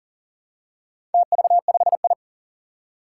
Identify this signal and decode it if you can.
Morse code, heard as a keyed tone, about 40 wpm, 700 Hz. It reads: TV5I
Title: Four Challenging Letter-Number Combinations 40wpm